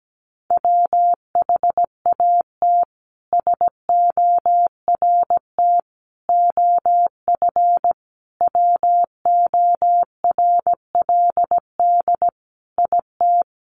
Morse code, heard as a keyed tone, about 17 wpm, 700 Hz. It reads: WHAT SORT OF WORLD IT
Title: A Sound Of Thunder 17wpm